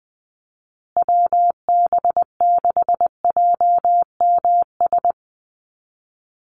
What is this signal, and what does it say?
Morse code, heard as a keyed tone, about 20 wpm, 700 Hz. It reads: WB6JMS
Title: USA Call Signs Repeated 2x 20wpm